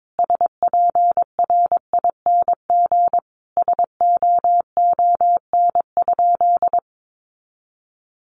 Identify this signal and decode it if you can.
Morse code, heard as a keyed tone, about 22 wpm, 700 Hz. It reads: SPRING SOON?
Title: Sentences from Top 1200 Words 22wpm